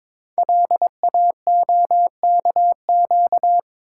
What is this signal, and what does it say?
Morse code, heard as a keyed tone, about 22 wpm, 700 Hz. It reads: LAOKQ
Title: Code Groups - 5 Letters - 22wpm